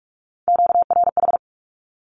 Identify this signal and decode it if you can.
Morse code, heard as a keyed tone, about 44 wpm, 700 Hz. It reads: QRH
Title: Q-Codes 45wpm